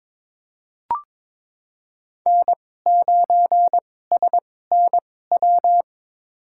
Morse code, heard as a keyed tone, about 22 wpm, spaced as for 16 wpm, 700 Hz. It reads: N9SNW
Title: USA Call Signs - 5 Characters - 2x Character Spacing - 22wpm